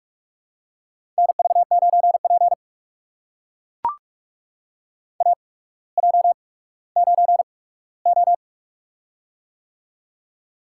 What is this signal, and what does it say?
Morse code, heard as a keyed tone, about 44 wpm, spaced as for 19 wpm, 690 Hz. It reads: NV9P AJ9O
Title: USA Call Signs - 4 Characters - 8x Character Spacing - 45wpm